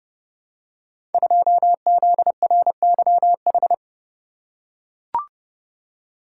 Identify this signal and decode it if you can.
Morse code, heard as a keyed tone, about 29 wpm, 700 Hz. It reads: 2ZRYH
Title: Code Groups - 5 Letter:Numbers - 30wpm